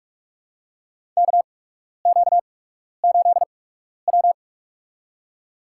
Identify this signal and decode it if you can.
Morse code, heard as a keyed tone, about 44 wpm, spaced as for 12 wpm, 700 Hz. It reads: KQ8W
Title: USA Call Signs - 4 Characters - 8x Character Spacing - 45wpm